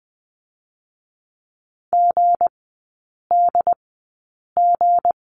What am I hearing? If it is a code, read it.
Morse code, heard as a keyed tone, about 20 wpm, spaced as for 8 wpm, 700 Hz. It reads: GDG
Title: 159 Lesson 26e - Letter g - Three Characters 20wpm